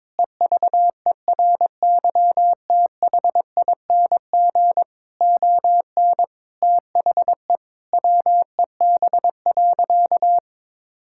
Morse code, heard as a keyed tone, about 22 wpm, 700 Hz. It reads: EVERYTHING ON THE WEB.
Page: morse-code-ninja-practice-22wpm